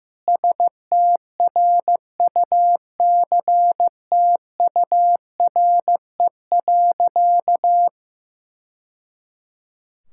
Morse code, heard as a keyed tone, about 16 wpm, 700 Hz. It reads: STRUCTURE.
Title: Sentences from Top 1000 Words - Review Part 6 - 15wpm